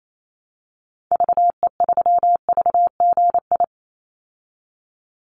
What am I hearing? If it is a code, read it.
Morse code, heard as a keyed tone, about 28 wpm, 700 Hz. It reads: VE3VGI
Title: Call Signs Repeated 3x - 28wpm